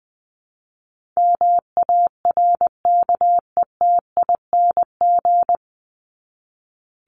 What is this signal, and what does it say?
Morse code, heard as a keyed tone, about 20 wpm, 700 Hz. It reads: MARKETING
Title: Top 751-1000 Words 20wpm